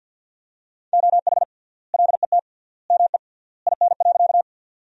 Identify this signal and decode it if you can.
Morse code, heard as a keyed tone, about 48 wpm, 700 Hz. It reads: OH LET ME IN.
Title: 3x Word-Spacing - Sentences from Top 500 Words - 50wpm